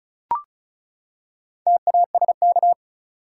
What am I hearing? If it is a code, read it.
Morse code, heard as a keyed tone, about 34 wpm, 700 Hz. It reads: TASK
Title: Four Letter Words - 35wpm